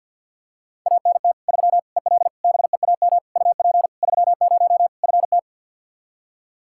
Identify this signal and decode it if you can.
Morse code, heard as a keyed tone, about 48 wpm, 700 Hz. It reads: ANT 3 EL BEAM UP 30 FT
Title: Practice QSO - Exchange 2 - Info - 50wpm